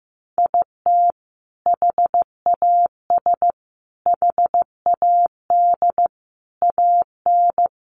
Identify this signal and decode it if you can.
Morse code, heard as a keyed tone, about 15 wpm, 700 Hz. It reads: IT HAS HAD AN
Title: Sentences from Top 1100 Words 15wpm